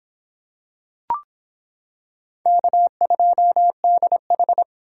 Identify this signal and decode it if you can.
Morse code, heard as a keyed tone, about 26 wpm, 700 Hz. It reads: K2DH